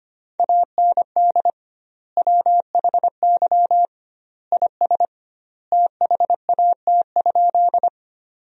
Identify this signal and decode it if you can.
Morse code, heard as a keyed tone, about 25 wpm, 700 Hz. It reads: AND WHY IS THAT?
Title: 2x Word-Spacing - Sentences from Top 200 Words - 25wpm